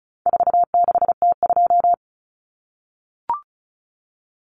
Morse code, heard as a keyed tone, about 34 wpm, 700 Hz. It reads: 46T2
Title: Common Cut Numbers - 4-Digit 35wpm